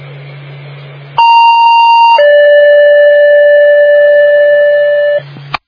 Two-tone page, heard as a one-second tone, then a three-second tone, about 930 then 580 Hz. It is heard from 1.2 to 5.2 s.